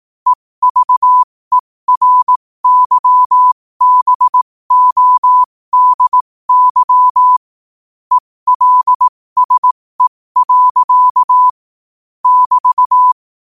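Morse code, heard as a keyed tone, about 18 wpm, spaced as for 15 wpm, 1000 Hz. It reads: EVERYBODY ELSE. <BT>
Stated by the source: Morse quote-of-the-day practice at 15 WPM from Morse Resource.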